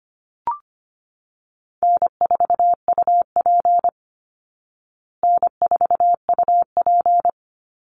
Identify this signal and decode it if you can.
Morse code, heard as a keyed tone, about 25 wpm, 700 Hz. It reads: N4UP N4UP
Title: Easy Call Signs Repeated 3x - 25wpm